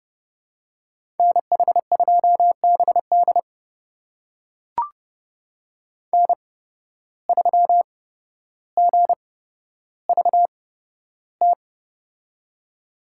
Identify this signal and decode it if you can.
Morse code, heard as a keyed tone, about 30 wpm, spaced as for 13 wpm, 700 Hz. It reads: NH2BD N3GVT
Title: USA Call Signs - 5 Characters - 8x Character Spacing - 30wpm